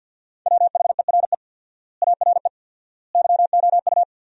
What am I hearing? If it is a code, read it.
Morse code, heard as a keyed tone, about 53 wpm, 700 Hz. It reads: WHERE ARE YOU
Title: 4x Word-Spacing - Sentences from Top 200 Words - 50wpm